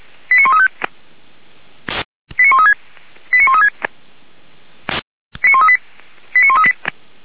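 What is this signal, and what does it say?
Five-tone selective call (ZVEI), heard as quick runs of five tones, tones of about 80 ms. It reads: 89135 89136 89136 89138 89138
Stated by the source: Five-tone ZVEI selective call received on SDR.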